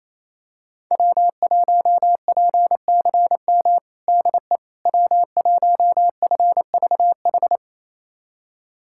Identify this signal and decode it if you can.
Morse code, heard as a keyed tone, about 28 wpm, 700 Hz. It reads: W1PCM DE W1FVH
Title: USA Callsign DE Callsign - 5 Characters - 28wpm